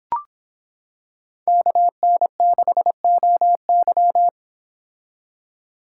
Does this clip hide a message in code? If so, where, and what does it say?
Yes: Morse code from 1.5 to 4.3 s, heard as a keyed tone, about 26 wpm, 700 Hz. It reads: KN6OY